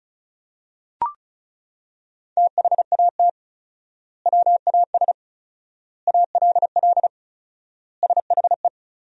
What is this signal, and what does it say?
Morse code, heard as a keyed tone, about 34 wpm, 700 Hz. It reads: THAT WAS ALL SHE
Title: 4x Word-Spacing - Sentences from Top 100 Words - 35wpm